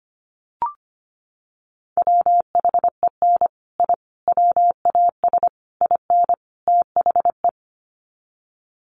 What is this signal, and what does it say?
Morse code, heard as a keyed tone, about 25 wpm, 700 Hz. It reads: WHEN I WAS IN THE
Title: Sets of 5 Words Encore2 25wpm